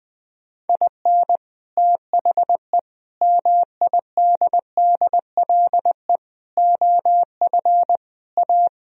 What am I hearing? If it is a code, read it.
Morse code, heard as a keyed tone, about 20 wpm, 700 Hz. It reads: IN THE MIDDLE OF A
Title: Speed-Racing - Sets of 5 Words 17wpm